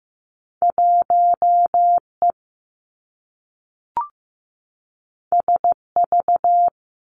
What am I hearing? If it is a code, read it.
Morse code, heard as a keyed tone, about 15 wpm, 700 Hz. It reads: 1 E S V
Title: Two Challenging Letter-Number Combinations 15wpm